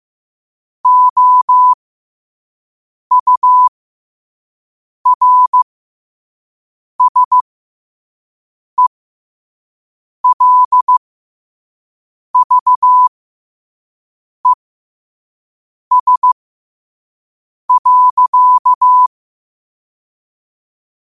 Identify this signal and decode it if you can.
Morse code, heard as a keyed tone, about 15 wpm, spaced as for 5 wpm, 1000 Hz. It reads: OURSELVES.
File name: 5wpm